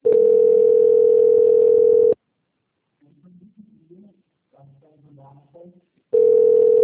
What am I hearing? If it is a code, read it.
Ringback tone (North America), 440 + 480 Hz, with speech.
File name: out- - - - - .2986